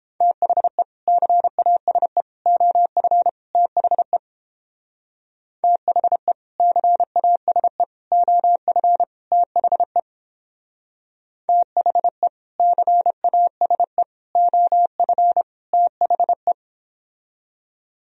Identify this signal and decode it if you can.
Morse code, heard as a keyed tone, about 29 wpm, 700 Hz. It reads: THE CASE OF THE THE CASE OF THE THE CASE OF THE